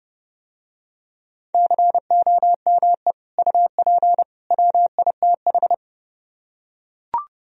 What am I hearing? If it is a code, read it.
Morse code, heard as a keyed tone, about 27 wpm, 700 Hz. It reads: COME UP WITH